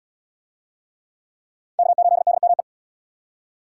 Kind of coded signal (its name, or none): Morse code